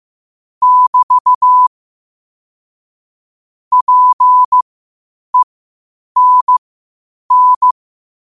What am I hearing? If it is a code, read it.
Morse code, heard as a keyed tone, about 15 wpm, spaced as for 8 wpm, 1000 Hz. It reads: <BT> PENN
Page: New quotes every day in morse code at 8 Words per minute.